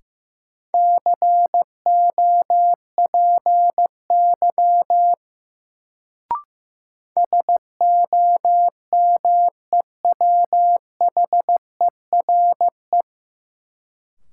Morse code, heard as a keyed tone, about 15 wpm, 700 Hz. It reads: COPY SOMEWHERE
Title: Sentences from Top 1400 Words - New Word Drill - 15wpm